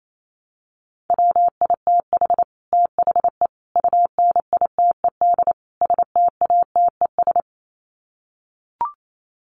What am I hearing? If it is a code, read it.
Morse code, heard as a keyed tone, about 28 wpm, 700 Hz. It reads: WITH THE UNITED STATES
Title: Sets of 4 Words 28wpm